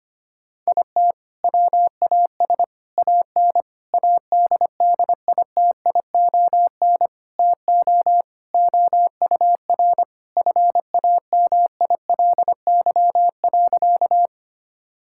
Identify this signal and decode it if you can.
Morse code, heard as a keyed tone, about 25 wpm, 700 Hz. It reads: IT WAS AN ADDITION TO OUR FAMILY.
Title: Sentences from Top 1000 Words - Review Part 6 - 25wpm